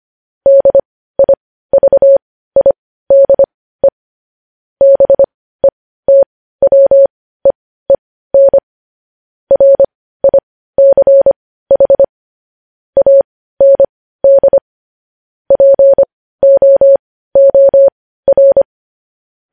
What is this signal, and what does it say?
Morse code, heard as a keyed tone, about 25 wpm, spaced as for 15 wpm, 550 Hz. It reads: DIVIDE BETWEEN RICH AND POOR
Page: news-headlines-in-morse-code-at-15-wpm